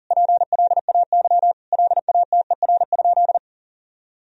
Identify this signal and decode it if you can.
Morse code, heard as a keyed tone, about 40 wpm, 690 Hz. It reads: PLAY LATER?
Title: Sentences from Top 1000 Words - Review Part 4 - 40wpm